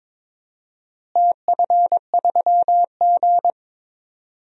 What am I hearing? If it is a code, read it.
Morse code, heard as a keyed tone, about 22 wpm, 700 Hz. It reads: TF3G
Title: DXCC-100 Call Signs - 4 Characters - Repeated 2x - 22wpm